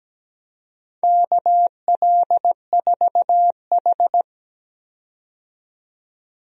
Morse code, heard as a keyed tone, about 17 wpm, 700 Hz. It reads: KL4H
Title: USA Call Signs - 4 Characters - 17wpm